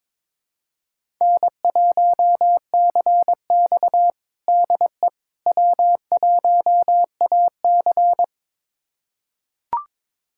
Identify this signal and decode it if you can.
Morse code, heard as a keyed tone, about 22 wpm, 700 Hz. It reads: N1CX DE W1AC
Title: USA Callsign DE Callsign - 4 Characters - 22wpm